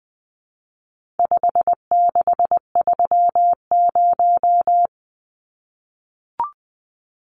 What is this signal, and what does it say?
Morse code, heard as a keyed tone, about 20 wpm, 700 Hz. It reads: 5630
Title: Numbers - 4-Digit 20wpm